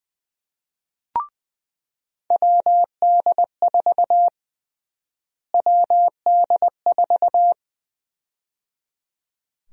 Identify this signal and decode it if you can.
Morse code, heard as a keyed tone, about 20 wpm, 700 Hz. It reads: WD4 WD4
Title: USA Call Sign Prefixes - Repeated 2x - 20wpm